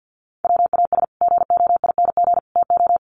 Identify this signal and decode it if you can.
Morse code, heard as a keyed tone, about 50 wpm, 700 Hz. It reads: WAS GOING TO
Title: Sets of 3 Words 50wpm